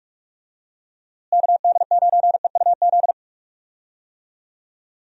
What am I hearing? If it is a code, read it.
Morse code, heard as a keyed tone, about 41 wpm, 700 Hz. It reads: KD9EUZ
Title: USA Call Signs - 6 Characters - Repeated 3x - 45wpm